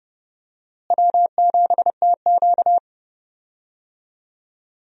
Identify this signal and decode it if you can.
Morse code, heard as a keyed tone, about 30 wpm, 700 Hz. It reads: W7TQ
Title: USA Call Signs - 4 Characters - Repeated 3x - 30wpm